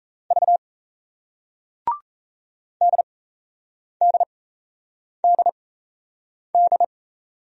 Morse code, heard as a keyed tone, about 38 wpm, spaced as for 8 wpm, 700 Hz. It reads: V DDDD